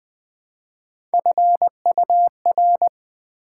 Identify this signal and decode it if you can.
Morse code, heard as a keyed tone, about 20 wpm, 700 Hz. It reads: FUR
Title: Build A Word - 1 Letter At A Time - 3-Letter Words - 20wpm